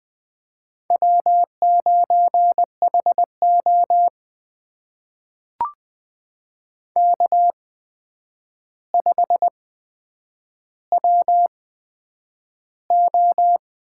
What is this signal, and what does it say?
Morse code, heard as a keyed tone, about 20 wpm, spaced as for 9 wpm, 700 Hz. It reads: W9HO K5WO
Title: USA Call Signs - 4 Characters - 8x Character Spacing - 20wpm